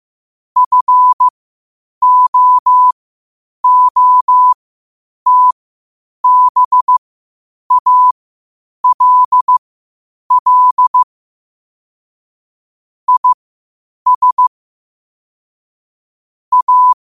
Morse code, heard as a keyed tone, about 15 wpm, spaced as for 8 wpm, 1000 Hz. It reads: FOOTBALL IS A
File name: qotdfiles / 8wpm